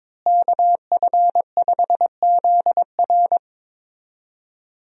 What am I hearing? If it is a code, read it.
Morse code, heard as a keyed tone, about 22 wpm, 700 Hz. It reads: KF5ZR
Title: USA Call Signs - 5 Characters - Repeated 2x - 22wpm